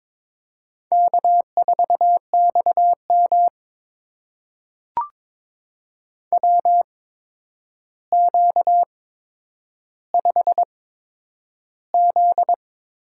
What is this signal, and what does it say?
Morse code, heard as a keyed tone, about 22 wpm, spaced as for 9 wpm, 700 Hz. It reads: K4XM WQ5Z
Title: USA Call Signs - 4 Characters - 8x Character Spacing - 22wpm